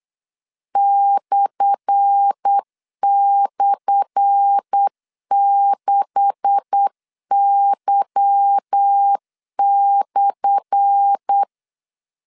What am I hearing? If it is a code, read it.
Morse code, heard as a keyed tone, about 8 wpm, 800 Hz. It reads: //6Y/
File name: LEZIONI / 10 SIGNS / 10A SLASH 8wpm 3452